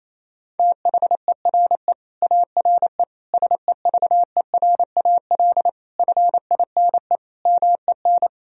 Morse code, heard as a keyed tone, about 28 wpm, 700 Hz. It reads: THERE ARE SEVERAL FINE MEN
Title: Sentences from Top 1000 Words - Review Part 2 - 28wpm